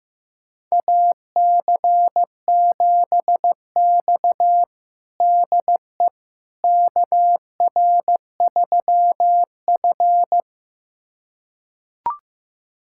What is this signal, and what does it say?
Morse code, heard as a keyed tone, about 16 wpm, 700 Hz. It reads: AC7X DE KR3F
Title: USA Callsign DE Callsign - 4 Characters - 15wpm